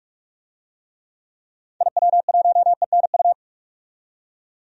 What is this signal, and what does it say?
Morse code, heard as a keyed tone, about 46 wpm, 700 Hz. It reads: IW1ENU
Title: DXCC-100 Call Signs - 6 Characters - 45wpm